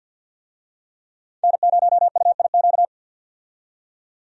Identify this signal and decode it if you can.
Morse code, heard as a keyed tone, about 48 wpm, 700 Hz. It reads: N0UIQ
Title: USA Call Signs - 5 Characters - 50wpm